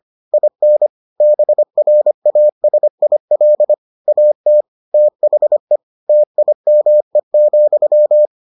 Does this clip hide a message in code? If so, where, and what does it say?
Yes: Morse code from 0.3 to 8.3 s, heard as a keyed tone, about 25 wpm, 600 Hz. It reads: IN BRASIL AT THE TIME,